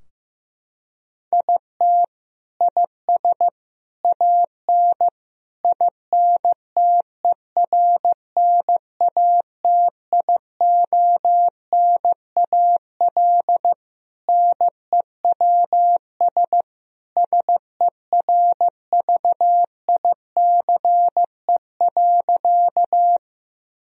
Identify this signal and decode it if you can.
Morse code, heard as a keyed tone, about 15 wpm, 700 Hz. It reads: IT IS AN INTERNATIONAL NEWS SERVICE.
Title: Sentences from Top 1000 Words - Review Part 4 - 15wpm